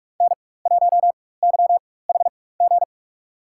Morse code, heard as a keyed tone, about 47 wpm, spaced as for 21 wpm, 700 Hz. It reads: N1YHG
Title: USA Call Signs - 5 Characters - 4x Character Spacing - 45wpm